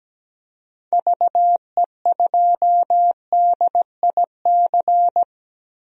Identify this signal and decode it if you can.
Morse code, heard as a keyed tone, about 17 wpm, 700 Hz. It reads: VE2DIC
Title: DXCC-100 Call Signs - 6 Characters - 17wpm